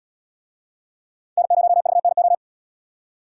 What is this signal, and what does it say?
Morse code, heard as a keyed tone, about 73 wpm, 700 Hz. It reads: N0VNO